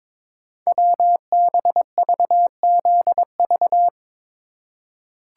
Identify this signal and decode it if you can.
Morse code, heard as a keyed tone, about 22 wpm, 700 Hz. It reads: WBVZV
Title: Code Groups - 5 Letters - 22wpm